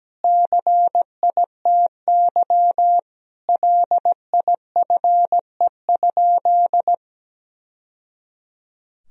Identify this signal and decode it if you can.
Morse code, heard as a keyed tone, about 17 wpm, 700 Hz. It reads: CITY LIFE?
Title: Sentences from Top 1000 Words - Review Part 1 - 17wpm